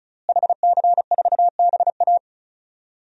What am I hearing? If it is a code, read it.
Morse code, heard as a keyed tone, about 34 wpm, 700 Hz. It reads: HC4BA
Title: DXCC-100 Call Signs - 5 Characters - Repeated 3x - 35wpm